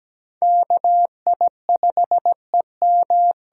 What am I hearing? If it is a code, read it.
Morse code, heard as a keyed tone, about 17 wpm, 700 Hz. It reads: KI5EM
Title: USA Call Signs - 6 Characters - 8x Character Spacing - 17wpm